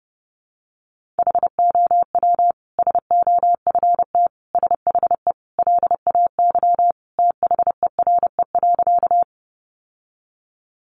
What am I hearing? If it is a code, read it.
Morse code, heard as a keyed tone, about 28 wpm, 700 Hz. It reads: HOW SOFT SHE LAY THERE.
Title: Sentences from Top 1500 Words 30wpm